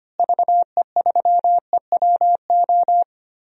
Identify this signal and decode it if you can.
Morse code, heard as a keyed tone, about 25 wpm, 700 Hz. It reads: VE3EWO
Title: DXCC-100 Call Signs - 6 Characters - Repeated 3x - 25wpm